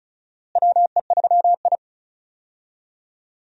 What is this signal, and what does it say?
Morse code, heard as a keyed tone, about 34 wpm, 700 Hz. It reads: WE3I
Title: USA Call Signs - 4 Characters - Repeated 3x - 35wpm